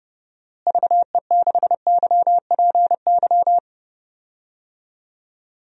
Morse code, heard as a keyed tone, about 32 wpm, 700 Hz. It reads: VE6YPY
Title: DXCC-100 Call Signs - 6 Characters - Repeated 2x - 30wpm